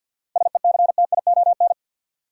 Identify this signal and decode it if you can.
Morse code, heard as a keyed tone, about 51 wpm, 700 Hz. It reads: SECTION
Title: Seven Letter Words - 50wpm